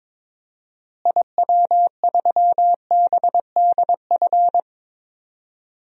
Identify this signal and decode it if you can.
Morse code, heard as a keyed tone, about 22 wpm, 700 Hz. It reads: IW3BDF
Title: DXCC-100 Call Signs - 6 Characters - Repeated 3x - 22wpm